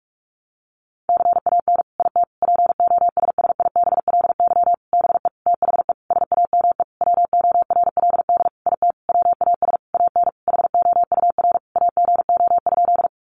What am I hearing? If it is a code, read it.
Morse code, heard as a keyed tone, about 44 wpm, 700 Hz. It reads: CAN IT POSSIBLY BE THE SAME WORLD IT WAS AN HOUR AGO?
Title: Sentences from Top 1300 Words 45wpm